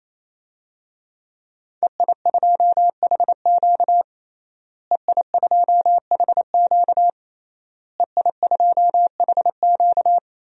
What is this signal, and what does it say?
Morse code, heard as a keyed tone, about 28 wpm, 700 Hz. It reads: EI2HQ EI2HQ EI2HQ
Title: DXCC-100 Call Signs - 5 Characters - Repeated 3x - 28wpm